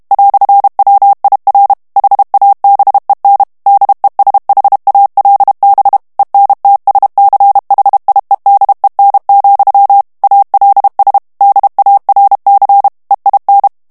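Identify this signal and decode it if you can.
Morse code, heard as a keyed tone, about 32 wpm, 800 Hz. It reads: "WIR HABEN DESHALB ENTSCHIEDEN, ALS DARC EIN
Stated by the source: German plain-text Morse reading practice at 30 words per minute.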